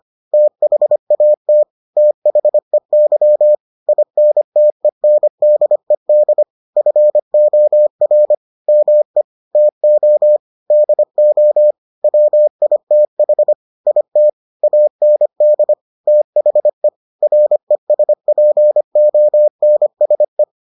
Morse code, heard as a keyed tone, about 25 wpm, 600 Hz. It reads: THAT THEY INTENDED FOR ME TO DO WITH IT AND THE RESPONSE